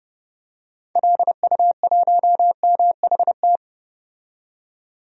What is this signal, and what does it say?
Morse code, heard as a keyed tone, about 30 wpm, 700 Hz. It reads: LU1MHT
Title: DXCC-100 Call Signs - 6 Characters - 30wpm